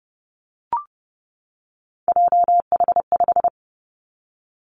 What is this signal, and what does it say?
Morse code, heard as a keyed tone, about 30 wpm, 700 Hz. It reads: JH5
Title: Three Challenging Letter-Number Combinations 30wpm